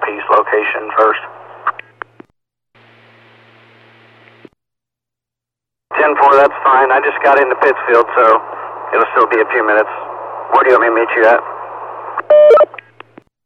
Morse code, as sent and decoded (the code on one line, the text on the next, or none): none